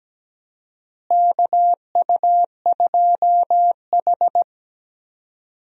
Morse code, heard as a keyed tone, about 17 wpm, 700 Hz. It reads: KU2H
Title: USA Call Signs - 4 Characters - Repeated 3x - 17wpm